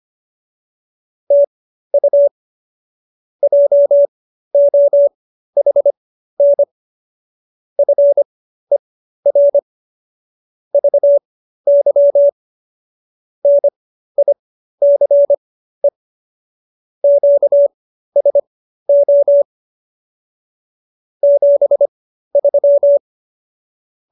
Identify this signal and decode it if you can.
Morse code, heard as a keyed tone, about 25 wpm, spaced as for 13 wpm, 570 Hz. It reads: TU JOHN FER VY NICE QSO 73